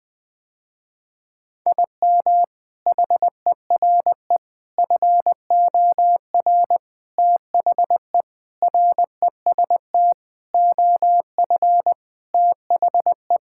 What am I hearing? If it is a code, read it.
Morse code, heard as a keyed tone, about 20 wpm, 700 Hz. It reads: IM HERE FOR THE REST OF THE
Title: Sentences with 4-Word Phrases - Spoken Beginning - 20wpm